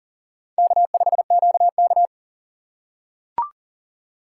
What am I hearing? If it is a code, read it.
Morse code, heard as a keyed tone, about 40 wpm, 700 Hz. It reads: K5QK
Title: USA Call Signs - 4 Characters - 2x Character Spacing - 40wpm